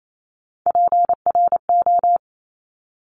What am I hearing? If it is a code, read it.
Morse code, heard as a keyed tone, about 28 wpm, 700 Hz. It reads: PRO
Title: Top Three Letter Combinations in Words 28wpm